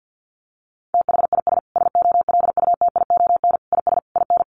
Morse code, heard as a keyed tone, about 50 wpm, 700 Hz. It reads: THIS SOLUTION IS IN
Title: Sentences from Top 1300 Words 50wpm